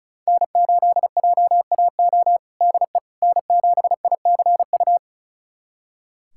Morse code, heard as a keyed tone, about 36 wpm, 700 Hz. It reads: N8JAO DE N7ICU
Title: USA Callsign DE Callsign 35wpm